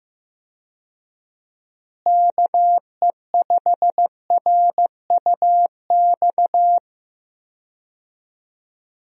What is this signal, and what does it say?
Morse code, heard as a keyed tone, about 15 wpm, 700 Hz. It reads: KE5RUX
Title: USA Call Signs - 6 Characters - Repeated 2x - 15wpm